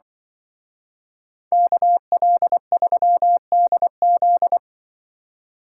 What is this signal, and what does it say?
Morse code, heard as a keyed tone, about 24 wpm, 700 Hz. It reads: KL3DZ
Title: Speed Racing - USA 2x2 Call Signs 20wpm